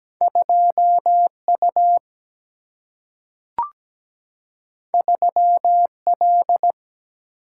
Morse code, heard as a keyed tone, about 17 wpm, 700 Hz. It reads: 2U 3L
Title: Single-Letter Single-Number 17wpm